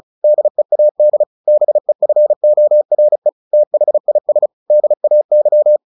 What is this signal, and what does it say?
Morse code, heard as a keyed tone, about 34 wpm, 600 Hz. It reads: DEAD BEFORE THIS DAY
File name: the old man and the sea / the old man and the sea 1 35wpm